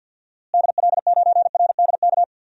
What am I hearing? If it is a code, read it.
Morse code, heard as a keyed tone, about 48 wpm, 700 Hz. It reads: DL9RDK
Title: DXCC-100 Call Signs - 6 Characters - Repeated 3x - 50wpm